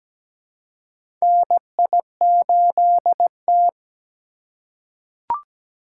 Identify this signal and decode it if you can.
Morse code, heard as a keyed tone, about 17 wpm, 700 Hz. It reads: NI8T
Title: USA Call Signs - 4 Characters - 4x Character Spacing - 17wpm